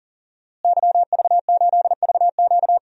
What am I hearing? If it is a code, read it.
Morse code, heard as a keyed tone, about 37 wpm, 700 Hz. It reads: YV8VQ
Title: DXCC-100 Call Signs - 5 Characters - Repeated 2x - 40wpm